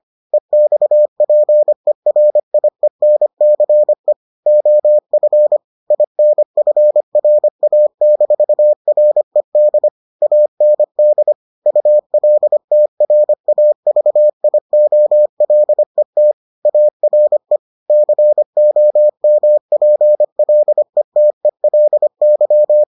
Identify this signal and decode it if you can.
Morse code, heard as a keyed tone, about 25 wpm, 600 Hz. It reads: EXPERIENCE OF INFRA-RED AND ULTRAVIOLET ARE COMPLETELY